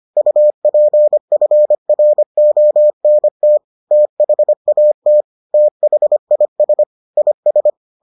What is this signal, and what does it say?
Morse code, heard as a keyed tone, about 25 wpm, 600 Hz. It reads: UPFRONT THAT THIS IS